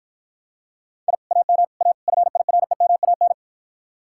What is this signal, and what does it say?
Morse code, heard as a keyed tone, about 53 wpm, 700 Hz. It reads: I AM A FIREMAN